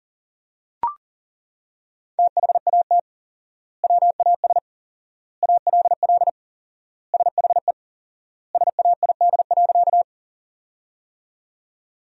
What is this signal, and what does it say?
Morse code, heard as a keyed tone, about 37 wpm, 700 Hz. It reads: THAT WAS ALL SHE SAID.
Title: 4x Word-Spacing - Sentences from Top 100 Words - 40wpm